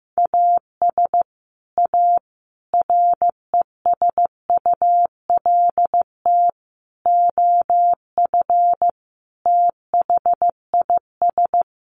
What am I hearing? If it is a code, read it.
Morse code, heard as a keyed tone, about 15 wpm, 700 Hz. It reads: AS A RESULT OF THIS
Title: Sets of 5 Words 15wpm